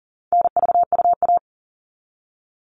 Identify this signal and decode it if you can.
Morse code, heard as a keyed tone, about 40 wpm, 700 Hz. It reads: NVUA
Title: All Cut Numbers - 4-Digit 40wpm